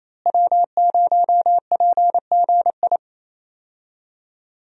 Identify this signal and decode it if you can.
Morse code, heard as a keyed tone, about 28 wpm, 700 Hz. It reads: W0PGI
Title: USA Call Signs - 5 Characters - Repeated 3x - 28wpm